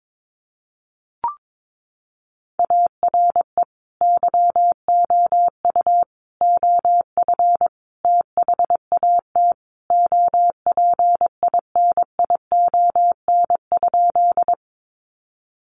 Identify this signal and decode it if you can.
Morse code, heard as a keyed tone, about 22 wpm, 700 Hz. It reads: ARE YOU OF THAT OPINION?
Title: Sentences from Top 1200 Words 22wpm